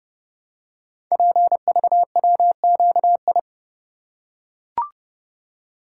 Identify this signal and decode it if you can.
Morse code, heard as a keyed tone, about 30 wpm, 700 Hz. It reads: PVWQI
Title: Code Groups - 5 Letters - 30wpm